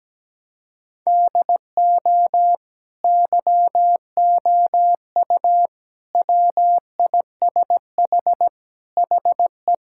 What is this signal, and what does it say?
Morse code, heard as a keyed tone, about 17 wpm, 700 Hz. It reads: DO YOU WISH HE
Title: Sentences from Top 1600 Words 17wpm